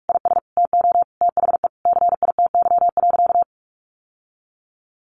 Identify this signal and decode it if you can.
Morse code, heard as a keyed tone, about 44 wpm, 700 Hz. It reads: IS TO THE CITY.
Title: Sentences from Top 1200 Words 45wpm